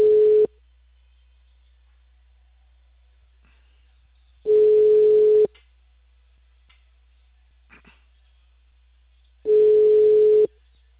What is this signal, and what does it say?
Ringback tone (Europe, 425 Hz).